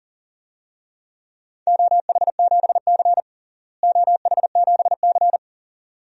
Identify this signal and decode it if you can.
Morse code, heard as a keyed tone, about 40 wpm, 700 Hz. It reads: OH7C OH7C
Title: DXCC-100 Call Signs - 4 Characters - Repeated 2x - 40wpm